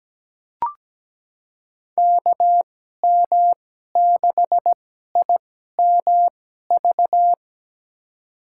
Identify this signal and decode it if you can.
Morse code, heard as a keyed tone, about 17 wpm, spaced as for 12 wpm, 700 Hz. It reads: KM6IMV
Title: USA Call Signs - 6 Characters - 2x Character Spacing - 17wpm